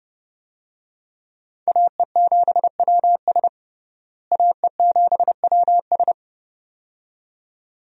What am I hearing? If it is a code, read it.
Morse code, heard as a keyed tone, about 30 wpm, 700 Hz. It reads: AE7WS AE7WS
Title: USA Call Signs - 5 Characters - Repeated 2x - 30wpm